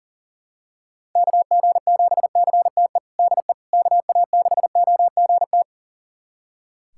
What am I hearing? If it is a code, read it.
Morse code, heard as a keyed tone, about 37 wpm, 700 Hz. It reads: KG7CTE DE KA6OGT
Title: USA Callsign DE Callsign - 6 Characters - 40wpm